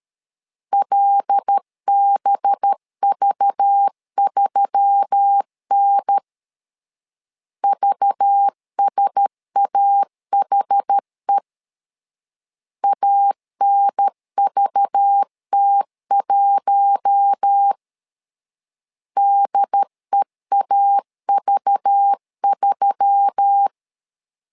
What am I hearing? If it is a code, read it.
Morse code, heard as a keyed tone, about 13 wpm, 800 Hz. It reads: LBV3N VSAHE ANVT1 DEAV3
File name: LEZIONI / 5 LBVC / 5 LBV 12wpm 1855